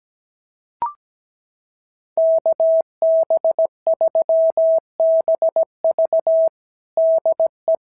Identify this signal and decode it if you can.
Morse code, heard as a keyed tone, about 17 wpm, 650 Hz. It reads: KB3BV DE
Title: SOTA QSO Practice 17wpm